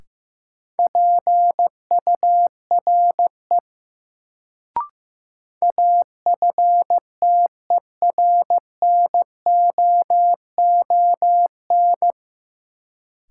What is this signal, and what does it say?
Morse code, heard as a keyed tone, about 15 wpm, 700 Hz. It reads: PURE AFTERNOON
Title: Sentences from Top 1500 Words - New Word Drill - 15wpm